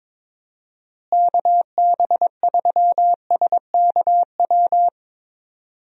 Morse code, heard as a keyed tone, about 22 wpm, 700 Hz. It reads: KB3SKW
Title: USA Call Signs - 6 Characters - Repeated 3x - 22wpm